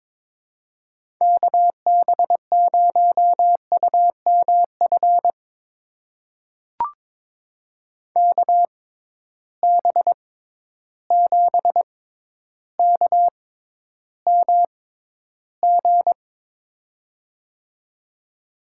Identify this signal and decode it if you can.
Morse code, heard as a keyed tone, about 23 wpm, spaced as for 11 wpm, 700 Hz. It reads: KB0UMF KB7KMG
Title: USA Call Signs - 6 Characters - 6x Character Spacing - 22wpm